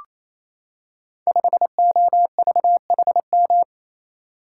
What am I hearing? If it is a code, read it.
Morse code, heard as a keyed tone, about 27 wpm, 700 Hz. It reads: 5OVHM
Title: Code Groups - 5 Letter:Numbers - 28wpm